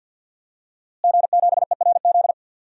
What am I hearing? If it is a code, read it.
Morse code, heard as a keyed tone, about 46 wpm, 700 Hz. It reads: G7ERZ
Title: DXCC-100 Call Signs - 5 Characters - Repeated 3x - 50wpm